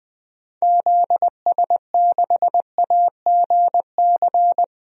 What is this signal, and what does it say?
Morse code, heard as a keyed tone, about 20 wpm, 700 Hz. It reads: ZS6AGC
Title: DXCC-100 Call Signs - 6 Characters - 20wpm